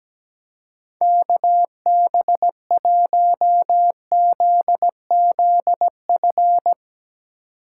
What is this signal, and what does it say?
Morse code, heard as a keyed tone, about 17 wpm, 700 Hz. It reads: KB1ZZF
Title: USA Call Signs - 2x Character Spacing - 17wpm